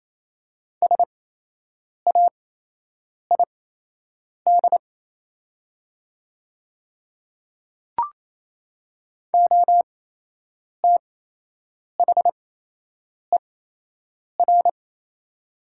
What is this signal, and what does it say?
Morse code, heard as a keyed tone, about 28 wpm, spaced as for 8 wpm, 700 Hz. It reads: SAID OTHER
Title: 8x Character Spacing - Top 100 Words - 28wpm